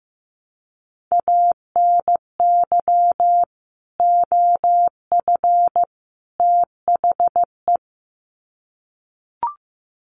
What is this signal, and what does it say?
Morse code, heard as a keyed tone, about 15 wpm, 700 Hz. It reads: ANY OF THE
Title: Sets of 3 Words 15wpm